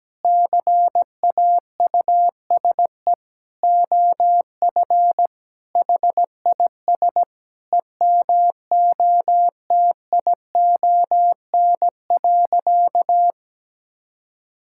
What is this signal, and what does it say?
Morse code, heard as a keyed tone, about 17 wpm, 700 Hz. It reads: CAUSE OF HIS EMOTION.
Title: Sentences from Top 1600 Words 17wpm